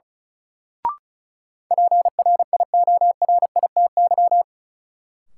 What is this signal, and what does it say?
Morse code, heard as a keyed tone, about 36 wpm, 700 Hz. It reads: PRIORITY
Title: Sentences from Top 1600 Words - New Word Drill Alt - 35wpm